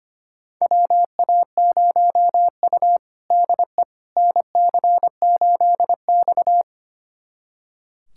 Morse code, heard as a keyed tone, about 25 wpm, 700 Hz. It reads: WA0U DE NC8X
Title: USA Callsign DE Callsign - 4 Characters - 25wpm